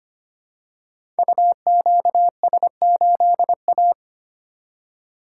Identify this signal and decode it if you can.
Morse code, heard as a keyed tone, about 25 wpm, 700 Hz. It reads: UQS8A
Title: Code Groups - 5 Letter:Numbers - 25wpm